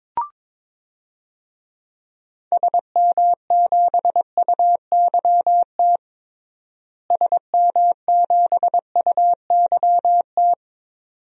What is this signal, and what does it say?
Morse code, heard as a keyed tone, about 22 wpm, 700 Hz. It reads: SM7UYT SM7UYT
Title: DXCC-100 Call Signs - 6 Characters - Repeated 2x - 22wpm